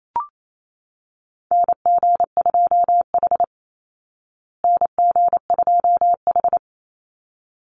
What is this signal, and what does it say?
Morse code, heard as a keyed tone, about 28 wpm, 700 Hz. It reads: NG2H NG2H
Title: Call Signs Repeated 3x - 28wpm